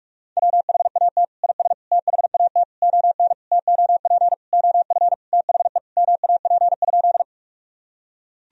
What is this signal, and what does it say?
Morse code, heard as a keyed tone, about 46 wpm, 700 Hz. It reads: WHAT IS THAT ON TOP OF THE MAP?
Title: Sentences from Top 1000 Words - Review Part 2 - 45wpm